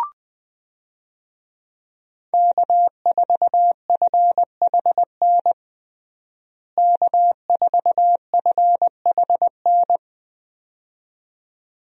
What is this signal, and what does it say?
Morse code, heard as a keyed tone, about 20 wpm, 700 Hz. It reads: K4FHN K4FHN
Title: USA Call Signs - 5 Characters - Repeated 2x - 20wpm